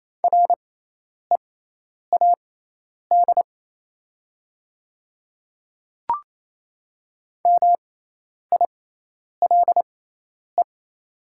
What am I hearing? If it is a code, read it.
Morse code, heard as a keyed tone, about 28 wpm, spaced as for 10 wpm, 700 Hz. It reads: READ MILE
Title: 6x Character Spacing - Top 300 Words - 28wpm